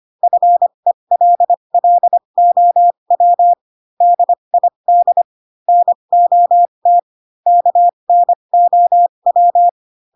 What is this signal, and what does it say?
Morse code, heard as a keyed tone, about 25 wpm, spaced as for 22 wpm, 700 Hz. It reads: FELLOW DID NOT KNOW
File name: Pinocchio / Pinocchio ch3 18 25 ik1hge